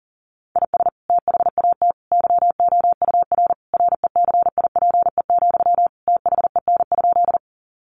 Morse code, heard as a keyed tone, about 40 wpm, 700 Hz. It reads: IS THAT YOUR RECIPE, THEN?
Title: Sentences from Top 1200 Words 40wpm